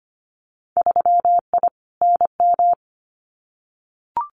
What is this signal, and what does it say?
Morse code, heard as a keyed tone, about 25 wpm, 700 Hz. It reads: 3I NM
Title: Winter Field Day Exchanges Encore 25wpm